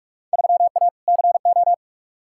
Morse code, heard as a keyed tone, about 46 wpm, 700 Hz. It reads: 3A CO
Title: Field Day Exchanges Encore - 45wpm